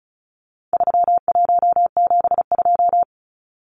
Morse code, heard as a keyed tone, about 34 wpm, 700 Hz. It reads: 3172
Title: Numbers - 4-Digit 35wpm